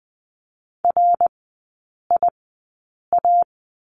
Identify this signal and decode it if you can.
Morse code, heard as a keyed tone, about 20 wpm, spaced as for 8 wpm, 700 Hz. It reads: RIA
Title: 159 Lesson 26e - Letter g - Three Characters 20wpm